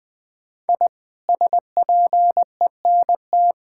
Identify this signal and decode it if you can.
Morse code, heard as a keyed tone, about 20 wpm, 700 Hz. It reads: I SPENT
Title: Sets of 6 Words - 20wpm